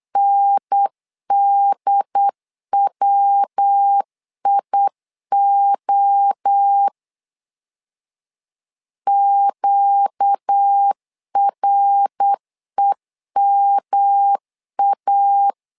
Morse code, heard as a keyed tone, about 8 wpm, 800 Hz. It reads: NDWIO QREMA
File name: LEZIONI / 4 Q 8wpm 2732